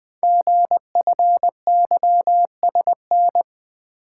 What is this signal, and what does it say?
Morse code, heard as a keyed tone, about 20 wpm, 700 Hz. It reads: GFYSN
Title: Code Groups - 5 Letter:Numbers - 20wpm